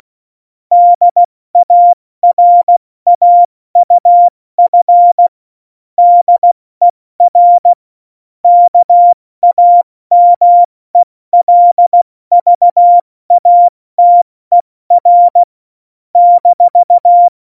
Morse code, heard as a keyed tone, about 16 wpm, 700 Hz. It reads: DARAUF DER KAMELVATER -
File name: Morsekurs / Witze / Witz 06. November 2018, 70 F80